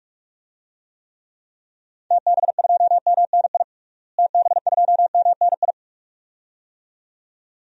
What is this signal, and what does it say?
Morse code, heard as a keyed tone, about 41 wpm, 700 Hz. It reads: TB2MNI TB2MNI
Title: DXCC-100 Call Signs - 6 Characters - Repeated 2x - 45wpm